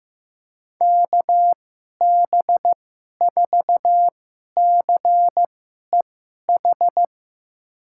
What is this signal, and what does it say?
Morse code, heard as a keyed tone, about 15 wpm, spaced as for 11 wpm, 700 Hz. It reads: KB4CEH
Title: USA Call Signs - 6 Characters - 2x Character Spacing - 15wpm